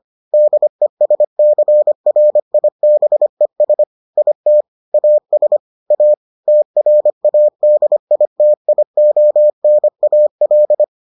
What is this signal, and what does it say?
Morse code, heard as a keyed tone, about 25 wpm, 600 Hz. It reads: DESCRIBES IT AS A TRADITIONAL